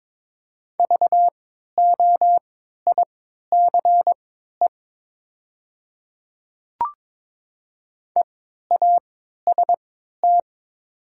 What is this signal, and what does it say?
Morse code, heard as a keyed tone, about 22 wpm, spaced as for 13 wpm, 690 Hz. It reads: VOICE EAST
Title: 3x Character Spacing - Top 500 Words - 22wpm